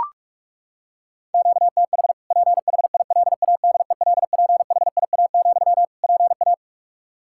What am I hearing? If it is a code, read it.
Morse code, heard as a keyed tone, about 41 wpm, 700 Hz. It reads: QTH PHILADELPHIA, PA
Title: Practice QSO - Exchange 1 - Beginning - 45wpm